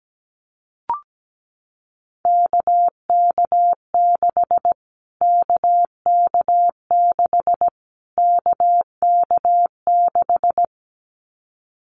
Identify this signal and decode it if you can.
Morse code, heard as a keyed tone, about 17 wpm, 700 Hz. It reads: KK6 KK6 KK6
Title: Callsign Prefixes for Top 25 DXCC - Repeated 3x - 17wpm